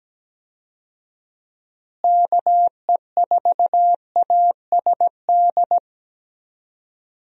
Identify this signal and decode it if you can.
Morse code, heard as a keyed tone, about 17 wpm, 700 Hz. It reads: KE4ASD
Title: USA Call Signs - 6 Characters - Repeated 2x - 17wpm